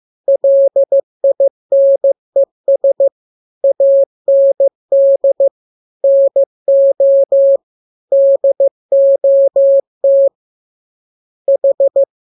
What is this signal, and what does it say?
Morse code, heard as a keyed tone, about 15 wpm, 550 Hz. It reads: LINES AND NO DOT H
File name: jokes / joke 20 15 0000